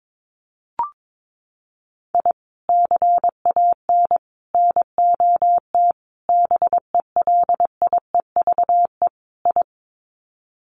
Morse code, heard as a keyed tone, about 22 wpm, 700 Hz. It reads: I CAN NOT BELIEVE I
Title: Sets of 5 Words Encore2 22wpm